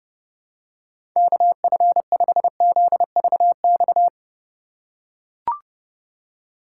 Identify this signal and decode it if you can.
Morse code, heard as a keyed tone, about 30 wpm, 700 Hz. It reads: KF5ZVX
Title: USA Call Signs - 6 Characters - 2x Character Spacing - 30wpm